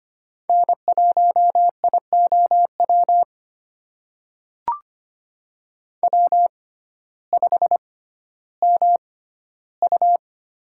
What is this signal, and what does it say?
Morse code, heard as a keyed tone, about 25 wpm, spaced as for 14 wpm, 700 Hz. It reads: N1IOW W5MU
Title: USA Call Signs - 6x Character Spacing - 25wpm